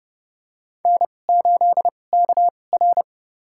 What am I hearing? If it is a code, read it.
Morse code, heard as a keyed tone, about 32 wpm, spaced as for 22 wpm, 700 Hz. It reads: N8KR
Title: USA Call Signs - 4 Characters - 2x Character Spacing - 30wpm